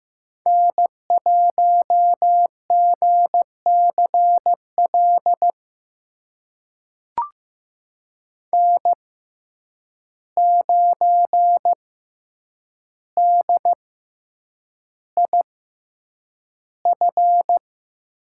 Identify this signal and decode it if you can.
Morse code, heard as a keyed tone, about 15 wpm, spaced as for 8 wpm, 700 Hz. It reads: N1GCL N9DIF
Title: USA Call Signs - 5 Characters - 6x Character Spacing - 15wpm